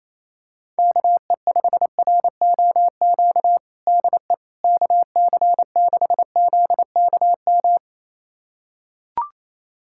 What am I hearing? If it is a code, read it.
Morse code, heard as a keyed tone, about 28 wpm, 700 Hz. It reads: KE5ROQ DE KC6ZKM
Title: USA Callsign DE Callsign - 6 Characters - 28wpm